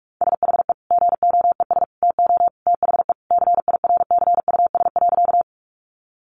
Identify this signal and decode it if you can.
Morse code, heard as a keyed tone, about 44 wpm, 700 Hz. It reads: SHE GOES TO THE CIRCUS.
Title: Easy Phrases 45wpm